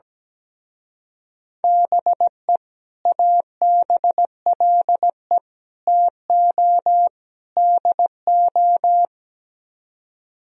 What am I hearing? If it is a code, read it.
Morse code, heard as a keyed tone, about 17 wpm, 700 Hz. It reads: BE ABLE TO DO
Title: Speed-Racing - Sets of 4 Words 17wpm